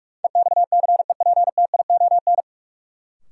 Morse code, heard as a keyed tone, about 46 wpm, 700 Hz. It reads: EXCEPTION
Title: Sets of 4 Words - New Word Drill - 45wpm